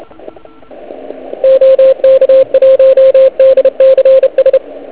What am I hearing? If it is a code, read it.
Morse code, heard as a keyed tone, about 32 wpm, 530 Hz, with radio noise. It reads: OK1DCS